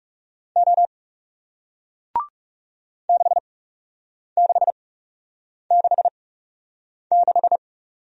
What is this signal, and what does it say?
Morse code, heard as a keyed tone, about 40 wpm, spaced as for 8 wpm, 700 Hz. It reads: O 6666